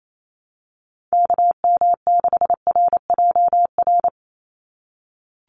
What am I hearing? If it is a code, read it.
Morse code, heard as a keyed tone, about 28 wpm, 700 Hz. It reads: KM6RJR
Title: USA Call Signs 28wpm